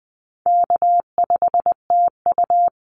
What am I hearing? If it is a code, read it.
Morse code, heard as a keyed tone, about 20 wpm, 700 Hz. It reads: K5TU
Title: Call Signs Repeated 2x - 20wpm